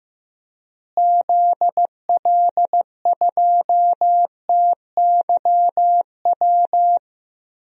Morse code, heard as a keyed tone, about 15 wpm, 700 Hz. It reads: ZL2TYW
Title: DXCC-100 Call Signs - 6 Characters - Repeated 2x - 15wpm